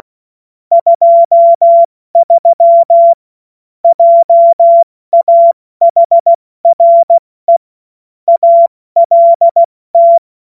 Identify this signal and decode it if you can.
Morse code, heard as a keyed tone, about 16 wpm, 680 Hz. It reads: 23 JAHRE ALT
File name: Morsekurs / Witze / Witz 14. September 2021, 70 F80